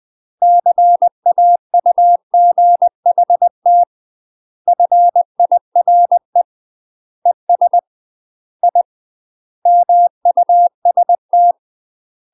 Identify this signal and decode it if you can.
Morse code, heard as a keyed tone, about 20 wpm, 700 Hz. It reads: CAUGHT FIRE ES I MUST
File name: MASTER QSOs TRAINING/20WPM